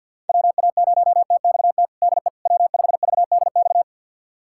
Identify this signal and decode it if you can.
Morse code, heard as a keyed tone, about 50 wpm, 700 Hz. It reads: WA0TXT DE W5VDX
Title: USA Callsign DE Callsign 50wpm